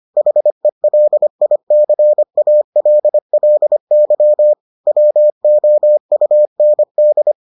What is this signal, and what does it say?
Morse code, heard as a keyed tone, about 25 wpm, 600 Hz. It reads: HELICALLY WOUND